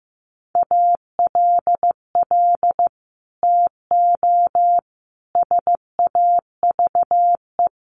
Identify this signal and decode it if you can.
Morse code, heard as a keyed tone, about 15 wpm, 700 Hz. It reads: ALL TO SAVE
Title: Sentences from Top 1100 Words 15wpm